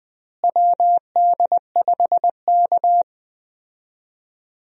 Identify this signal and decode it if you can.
Morse code, heard as a keyed tone, about 20 wpm, 700 Hz. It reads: WD5K
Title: USA Call Signs - 4 Characters - Repeated 3x - 20wpm